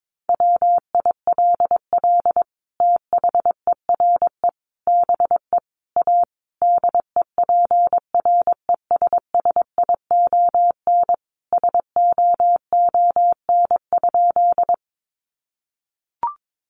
Morse code, heard as a keyed tone, about 22 wpm, 700 Hz. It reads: WILL THERE BE A DEPRESSION SOON?
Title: Sentences from Top 1300 Words 22wpm